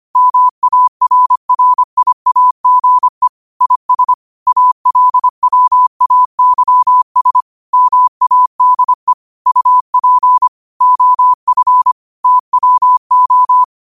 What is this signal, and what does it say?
Morse code, heard as a keyed tone, about 25 wpm, 1000 Hz. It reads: MARRIAGE IS ALWAYS MADE UP OF TWO
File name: qotdfiles / 25wpm